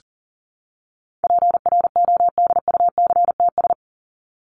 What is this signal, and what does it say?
Morse code, heard as a keyed tone, about 40 wpm, 700 Hz. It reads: PRODUCTS
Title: ES:S Suffix - Top 100 Words with Base Words 40wpm